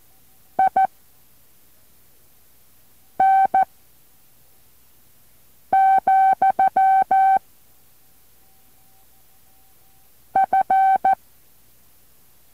Morse code, heard as a keyed tone, about 14 wpm, 770 Hz. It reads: I N , F